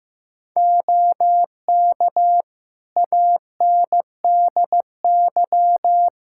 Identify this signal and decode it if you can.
Morse code, heard as a keyed tone, about 15 wpm, 700 Hz. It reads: OK ANDY
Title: Practice QSO - Exchange 2 - Info - 15wpm